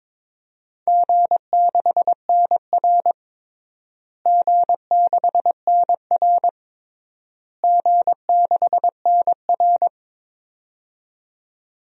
Morse code, heard as a keyed tone, about 22 wpm, 700 Hz. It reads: G6NR G6NR G6NR
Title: DXCC-100 Call Signs - 4 Characters - Repeated 3x - 22wpm